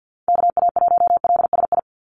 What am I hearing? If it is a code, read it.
Morse code, heard as a keyed tone, about 50 wpm, 700 Hz. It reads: KA1LSI
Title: USA Call Signs 50wpm